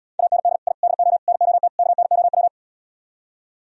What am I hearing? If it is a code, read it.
Morse code, heard as a keyed tone, about 77 wpm, 700 Hz. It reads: AND I DO NOT KNOW